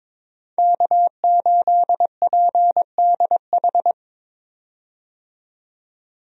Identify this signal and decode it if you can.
Morse code, heard as a keyed tone, about 22 wpm, 700 Hz. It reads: K8PDH
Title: USA Call Signs - 5 Characters - 22wpm